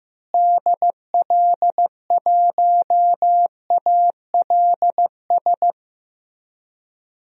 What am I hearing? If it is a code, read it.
Morse code, heard as a keyed tone, about 15 wpm, 700 Hz. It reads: DL1ALS
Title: DXCC-100 Call Signs - 6 Characters - Repeated 3x - 15wpm